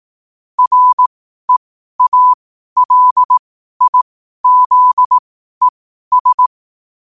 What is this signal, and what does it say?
Morse code, heard as a keyed tone, about 18 wpm, spaced as for 12 wpm, 1000 Hz. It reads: REALIZES
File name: qotdfiles / 12wpm